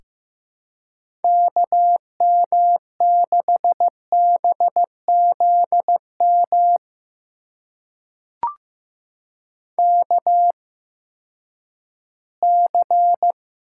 Morse code, heard as a keyed tone, about 15 wpm, spaced as for 10 wpm, 700 Hz. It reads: KM6BZM KC
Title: USA Call Signs - 6 Characters - 8x Character Spacing - 15wpm